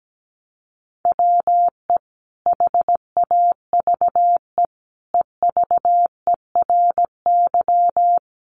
Morse code, heard as a keyed tone, about 17 wpm, 700 Hz. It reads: WE HAVE EVERY
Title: Sentences from Top 1100 Words 17wpm